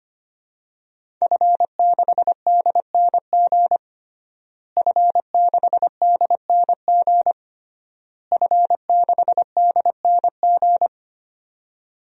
Morse code, heard as a keyed tone, about 25 wpm, 700 Hz. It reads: F6DNG F6DNG F6DNG
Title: DXCC-100 Call Signs - 5 Characters - Repeated 3x - 25wpm